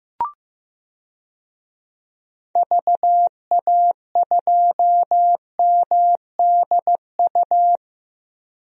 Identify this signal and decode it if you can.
Morse code, heard as a keyed tone, about 16 wpm, 700 Hz. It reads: VA2MDU
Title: DXCC-100 Call Signs - 6 Characters - Repeated 3x - 15wpm